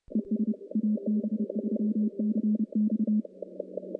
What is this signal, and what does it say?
Morse code, heard as a keyed tone, about 30 wpm, 210 Hz. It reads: ES AB3CX
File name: contests / CW-SS-k5tr- -1300